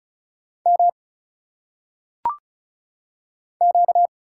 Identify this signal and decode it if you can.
Morse code, heard as a keyed tone, about 34 wpm, 700 Hz. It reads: M Q
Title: Single Letters - Phonetic Pronunciation 35wpm